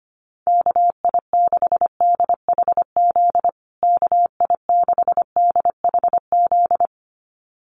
Morse code, heard as a keyed tone, about 25 wpm, 700 Hz. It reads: KI6DHZ KI6DHZ
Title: USA Call Signs Repeated 2x 25wpm